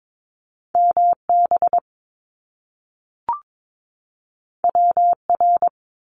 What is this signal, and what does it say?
Morse code, heard as a keyed tone, about 22 wpm, 700 Hz. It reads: MB WR
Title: Top Two Letter Combinations in Words 22wpm